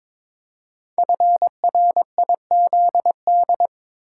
Morse code, heard as a keyed tone, about 22 wpm, 700 Hz. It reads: FRIZD
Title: Code Groups - 5 Letter:Numbers - 22wpm